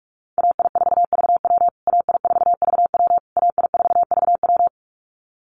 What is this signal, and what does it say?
Morse code, heard as a keyed tone, about 44 wpm, 700 Hz. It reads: AI4VW AI4VW AI4VW
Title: USA Call Signs Repeated 3x 45wpm